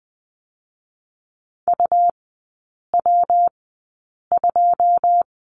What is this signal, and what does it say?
Morse code, heard as a keyed tone, about 20 wpm, 700 Hz. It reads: U W 2
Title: 114 Lesson 19e - Letter w - Three Characters 20wpm